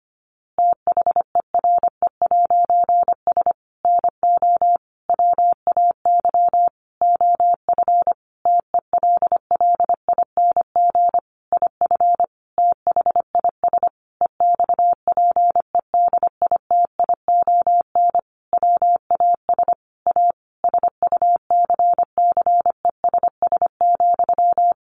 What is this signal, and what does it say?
Morse code, heard as a keyed tone, about 25 wpm, 700 Hz. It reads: THERE'S NO WAY OF TELLING IF THIS EXPEDITION WAS A SUCCESS,
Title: A Sound Of Thunder 25wpm